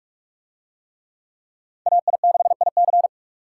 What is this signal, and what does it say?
Morse code, heard as a keyed tone, about 43 wpm, 700 Hz. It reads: AI6IC